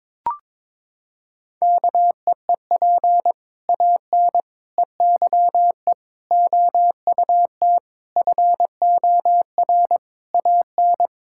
Morse code, heard as a keyed tone, about 22 wpm, 700 Hz. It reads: KEEP AN EYE OUT FOR AN
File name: Sentences with 7-Word Phrases - 22wpm